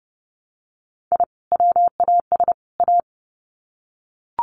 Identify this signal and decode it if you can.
Morse code, heard as a keyed tone, about 30 wpm, 700 Hz. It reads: I WAS A
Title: Sets of 3 Words 30wpm